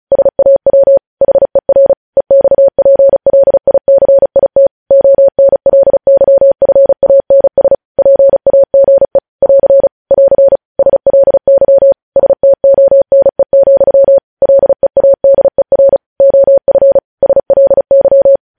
Morse code, heard as a keyed tone, about 36 wpm, 550 Hz. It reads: SAW HER EXPLICIT ONLYFANS PAGE <AR> <AR> SLY STONE, LEADER OF SLY
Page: news headlines in morse code at 35 wpm